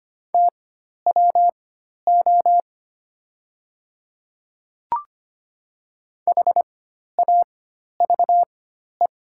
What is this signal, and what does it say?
Morse code, heard as a keyed tone, about 25 wpm, spaced as for 12 wpm, 700 Hz. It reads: TWO HAVE
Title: 4x Character Spacing - Top 100 Words - 25wpm